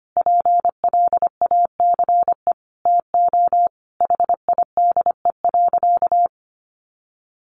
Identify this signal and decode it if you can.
Morse code, heard as a keyed tone, about 25 wpm, 700 Hz. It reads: PLACE TO HIDE.
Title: Easy Phrases 25wpm